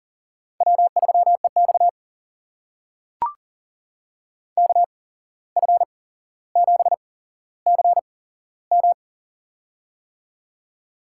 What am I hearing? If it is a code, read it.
Morse code, heard as a keyed tone, about 40 wpm, spaced as for 16 wpm, 700 Hz. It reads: W3EX KF7CM
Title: USA Call Signs - 8x Character Spacing - 40wpm